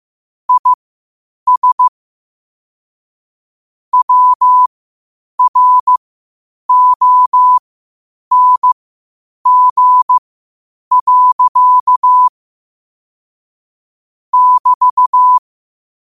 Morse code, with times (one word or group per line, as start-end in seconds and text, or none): 0.5-1.9 IS
3.9-12.3 WRONG.
14.3-15.4 <BT>